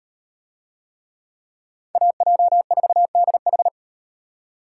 Morse code, heard as a keyed tone, about 35 wpm, 700 Hz. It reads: AJ4DH